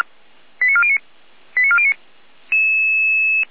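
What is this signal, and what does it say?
Five-tone selective call (ZVEI), heard as quick runs of five tones, tones of about 70 ms. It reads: 89409 89409
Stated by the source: ZVEI five-tone selective call received on an SDR.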